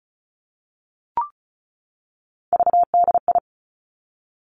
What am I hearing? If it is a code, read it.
Morse code, heard as a keyed tone, about 36 wpm, 700 Hz. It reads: VDI
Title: Three Challenging Letter-Number Combinations 35wpm